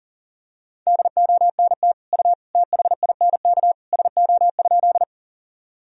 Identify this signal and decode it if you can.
Morse code, heard as a keyed tone, about 40 wpm, 700 Hz. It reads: DONT U THINK SO?
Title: Sentences from Top 300 Words and Abbreviations 40wpm